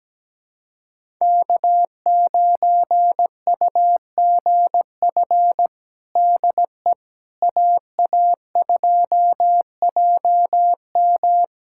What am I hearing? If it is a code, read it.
Morse code, heard as a keyed tone, about 17 wpm, 700 Hz. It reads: K9UGF DE AA2JM
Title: USA Callsign DE Callsign - 5 Characters - 17wpm